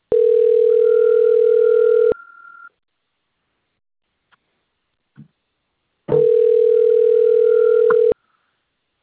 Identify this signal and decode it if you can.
Ringback tone (North America), 440 + 480 Hz.
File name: out- - - - - .25144